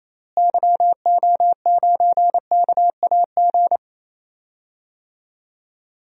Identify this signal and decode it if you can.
Morse code, heard as a keyed tone, about 28 wpm, 700 Hz. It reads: YO9KAG
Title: DXCC-100 Call Signs - 6 Characters - Repeated 3x - 28wpm